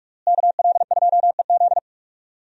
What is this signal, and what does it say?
Morse code, heard as a keyed tone, about 46 wpm, 700 Hz. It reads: KL2EZ
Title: USA Call Signs - 5 Characters - Repeated 3x - 45wpm